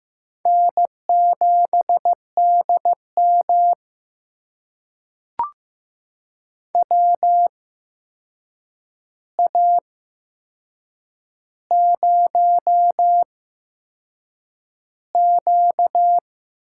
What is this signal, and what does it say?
Morse code, heard as a keyed tone, about 15 wpm, spaced as for 6 wpm, 700 Hz. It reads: N7DM WA0Q
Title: USA Call Signs - 4 Characters - 8x Character Spacing - 15wpm